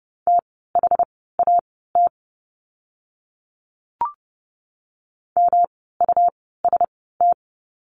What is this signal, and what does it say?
Morse code, heard as a keyed tone, about 30 wpm, spaced as for 17 wpm, 700 Hz. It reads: THAT MUST
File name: course / 30:10wpm / 237 Lesson 38f - Letter q - Words 30wpm